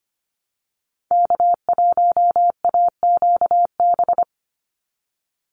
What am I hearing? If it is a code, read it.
Morse code, heard as a keyed tone, about 25 wpm, 700 Hz. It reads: K1AQB
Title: Call Signs Repeated 2x - 25wpm